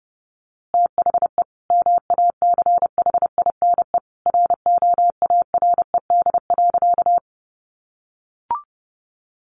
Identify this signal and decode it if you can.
Morse code, heard as a keyed tone, about 28 wpm, 700 Hz. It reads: THE MACHINE ROARED.
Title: A Sound Of Thunder 30wpm